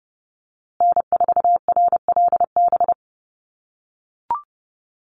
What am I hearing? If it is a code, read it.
Morse code, heard as a keyed tone, about 30 wpm, 700 Hz. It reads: N4RLB
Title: USA Call Signs Repeated 2x 30wpm